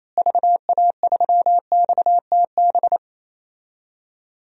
Morse code, heard as a keyed tone, about 27 wpm, 700 Hz. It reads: VA3XTB
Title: DXCC-100 Call Signs - 6 Characters - Repeated 3x - 28wpm